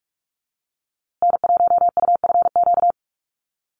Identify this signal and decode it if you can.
Morse code, heard as a keyed tone, about 48 wpm, 690 Hz. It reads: N1UFQ
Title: USA Call Signs Repeated 3x 45wpm